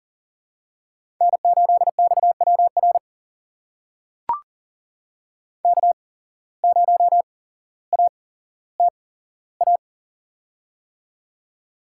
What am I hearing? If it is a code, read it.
Morse code, heard as a keyed tone, about 39 wpm, spaced as for 17 wpm, 700 Hz. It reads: N8XWR K0ATA
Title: USA Call Signs - 5 Characters - 8x Character Spacing - 40wpm